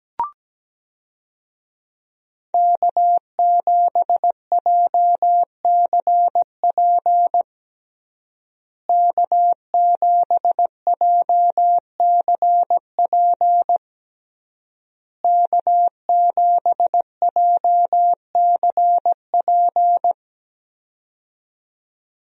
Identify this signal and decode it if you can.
Morse code, heard as a keyed tone, about 17 wpm, 700 Hz. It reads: K7JCP K7JCP K7JCP
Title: USA Call Signs - 5 Characters - Repeated 3x - 17wpm